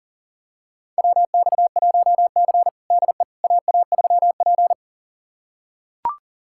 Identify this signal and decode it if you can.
Morse code, heard as a keyed tone, about 40 wpm, 700 Hz. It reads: WX1C DE AA3P
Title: USA Callsign DE Callsign - 4 Characters - 40wpm